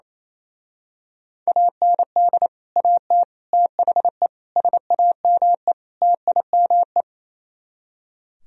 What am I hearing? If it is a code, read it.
Morse code, heard as a keyed tone, about 26 wpm, 700 Hz. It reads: AND AT THE SAME TIME
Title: Speed-Racing - Sets of 5 Words 28wpm